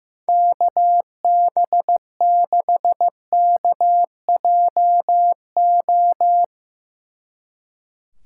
Morse code, heard as a keyed tone, about 17 wpm, 700 Hz. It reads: KB6KJO
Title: Speed Racing - USA 2x3 Call Signs 15wpm